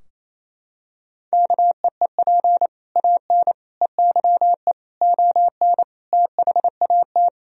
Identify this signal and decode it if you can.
Morse code, heard as a keyed tone, about 28 wpm, 700 Hz. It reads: KEEP AN EYE ON THAT
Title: Sentences from Top 1000 Words - Review Part 2 - 28wpm